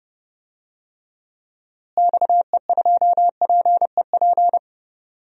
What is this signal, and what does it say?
Morse code, heard as a keyed tone, about 30 wpm, 700 Hz. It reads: XE2PEP
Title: DXCC-100 Call Signs - 6 Characters - Repeated 2x - 30wpm